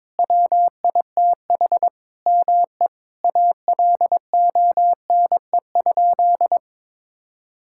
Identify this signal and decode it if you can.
Morse code, heard as a keyed tone, about 21 wpm, 700 Hz. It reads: WITH ME ALONE?
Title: Sentences from Top 1600 Words 22wpm